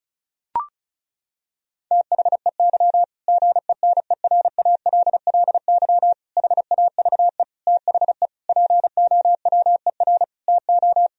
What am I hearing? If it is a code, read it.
Morse code, heard as a keyed tone, about 36 wpm, 700 Hz. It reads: THEY GENERALLY HAVE THE POWER TO
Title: Sentences from Top 1000 Words - Review Part 4 - 35wpm